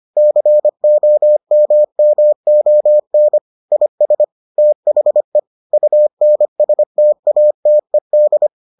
Morse code, heard as a keyed tone, about 25 wpm, 600 Hz. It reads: COMMON IS THE UNSTATED